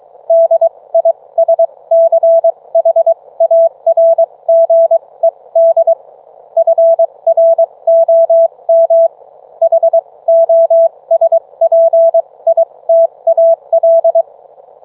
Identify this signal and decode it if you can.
Morse code, heard as a keyed tone, about 23 wpm, spaced as for 19 wpm, 660 Hz. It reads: DISCHARGED FROM HOSPITAL